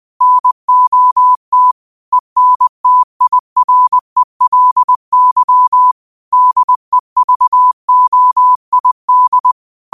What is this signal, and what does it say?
Morse code, heard as a keyed tone, about 20 wpm, 1000 Hz. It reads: NOT ENTIRELY DEVOID
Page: New quotes every day in morse code at 20 Words per minute.